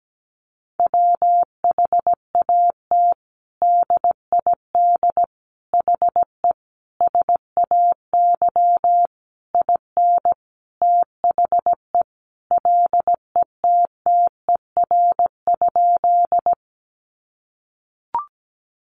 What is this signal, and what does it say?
Morse code, heard as a keyed tone, about 17 wpm, 700 Hz. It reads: WHAT DID HE SAY IN THE LETTER?
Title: Sentences from Top 300 Words 17wpm